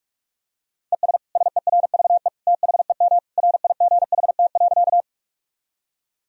Morse code, heard as a keyed tone, about 42 wpm, 700 Hz. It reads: ES SERVE THEM RIGHT.
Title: Sentences from Top 400 Words and Abbreviations 45wpm